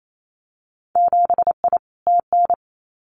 Morse code, heard as a keyed tone, about 28 wpm, 700 Hz. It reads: 7I TN
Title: Winter Field Day Exchanges Encore 28wpm